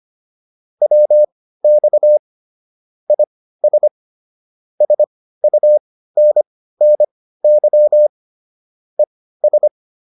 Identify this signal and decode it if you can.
Morse code, heard as a keyed tone, about 25 wpm, spaced as for 15 wpm, 600 Hz. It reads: WX IS SUNNY ES